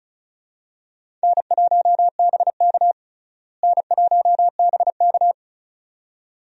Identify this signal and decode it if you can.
Morse code, heard as a keyed tone, about 34 wpm, 700 Hz. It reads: N1BK N1BK
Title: USA Call Signs - 4 Characters - Repeated 2x - 35wpm